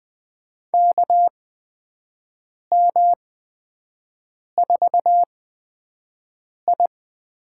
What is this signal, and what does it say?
Morse code, heard as a keyed tone, about 20 wpm, spaced as for 5 wpm, 700 Hz. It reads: KM4I
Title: USA Call Signs - 6 Characters - 8x Character Spacing - 20wpm